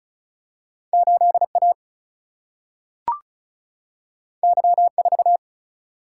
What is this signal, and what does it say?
Morse code, heard as a keyed tone, about 36 wpm, 700 Hz. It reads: 8A Y4
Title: Single-Letter Single-Number 35wpm